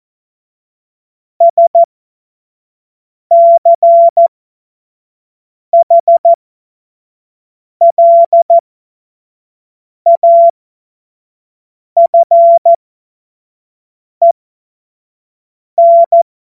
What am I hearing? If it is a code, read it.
Morse code, heard as a keyed tone, about 14 wpm, spaced as for 5 wpm, 680 Hz. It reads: SCHLAFEN
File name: Morsekurs / Witze / Witz 15. Februar 2022 25 F70